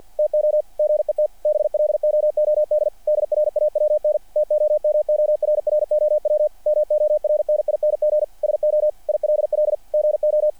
Morse code, heard as a keyed tone, about 48 wpm, 600 Hz. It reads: TO GET BLOOD DRAWN TOMORROW MORNING SO ILL GO
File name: archived / CRSnet